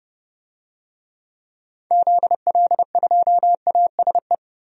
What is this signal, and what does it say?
Morse code, heard as a keyed tone, about 30 wpm, 700 Hz. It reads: ZL2ASE
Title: DXCC-100 Call Signs - 6 Characters - 30wpm